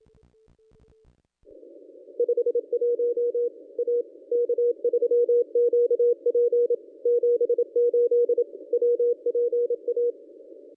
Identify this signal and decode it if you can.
Morse code, heard as a keyed tone, about 27 wpm, 480 Hz. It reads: 51 A K3QP 78 WPA
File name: contests / ss-cw-k5tr-op-n5rz -1000